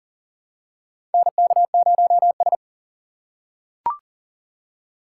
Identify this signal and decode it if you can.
Morse code, heard as a keyed tone, about 40 wpm, 700 Hz. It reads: NK0S
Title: USA Call Signs - 4 Characters - 4x Character Spacing - 40wpm